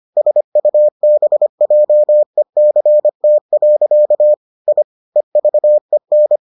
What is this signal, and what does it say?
Morse code, heard as a keyed tone, about 25 wpm, 600 Hz. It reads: SUBJECT. I EVEN